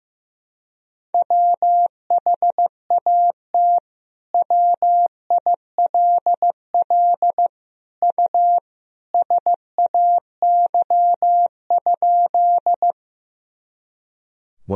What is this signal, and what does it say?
Morse code, heard as a keyed tone, about 15 wpm, 700 Hz. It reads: WHAT WILL U SAY?
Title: Sentences from Top 200 Words and Abbreviations 15wpm